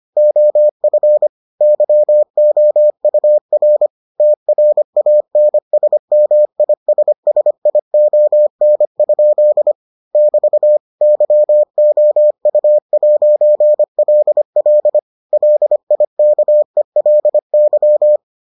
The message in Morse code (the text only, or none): OF YOUR TRANSMISSION? <BT> YOU'LL LIKELY